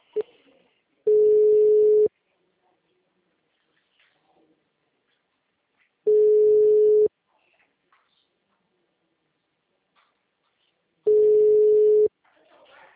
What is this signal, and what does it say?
Ringback tone (Europe, 425 Hz).
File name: out- -2014- - - .20